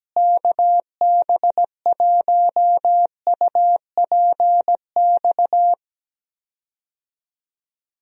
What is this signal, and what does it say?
Morse code, heard as a keyed tone, about 17 wpm, 700 Hz. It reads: KB1UPX
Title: USA Call Signs - 6 Characters - Repeated 3x - 17wpm